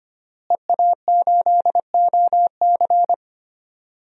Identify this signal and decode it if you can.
Morse code, heard as a keyed tone, about 25 wpm, 700 Hz. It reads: EA8OC
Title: DXCC-100 Call Signs - 5 Characters - Repeated 3x - 25wpm